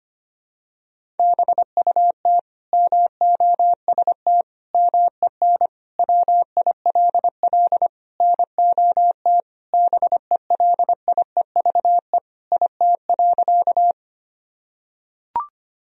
Morse code, heard as a keyed tone, about 25 wpm, 700 Hz. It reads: BUT MOST MEN WILL NOT BELIEVE IT.
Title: Sentences from Top 1000 Words - Review Part 1 - 25wpm